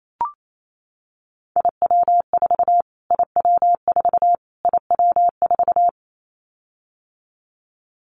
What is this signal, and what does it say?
Morse code, heard as a keyed tone, about 28 wpm, 700 Hz. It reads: IW4 IW4 IW4
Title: Callsign Prefixes for Top 25 DXCC - Repeated 3x - 28wpm